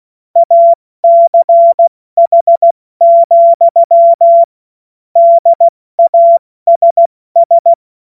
Morse code, heard as a keyed tone, about 16 wpm, 680 Hz. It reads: ACH, DASS
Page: cw-witze